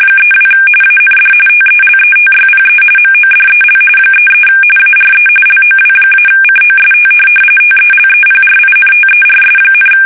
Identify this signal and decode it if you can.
RTTY, about 45.45 baud. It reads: 32 AND TELETYPE MODEL 33 ARE LOW-COST TELEPRINTERS,
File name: fsk850 / teletype-corp